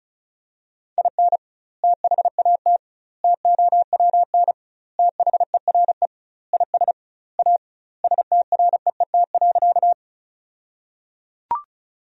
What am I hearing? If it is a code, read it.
Morse code, heard as a keyed tone, about 34 wpm, 700 Hz. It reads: IN THAT TOWN THERE IS A STREET.
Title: 2x Word-Spacing - Sentences from Top 500 Words - 35wpm